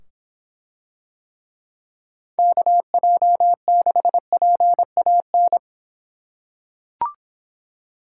Morse code, heard as a keyed tone, about 24 wpm, 700 Hz. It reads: KJ6PAN